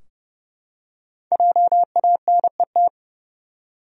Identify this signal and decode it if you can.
Morse code, heard as a keyed tone, about 30 wpm, 700 Hz. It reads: JANET
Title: USA Common Names - Max 5 Letters - 30wpm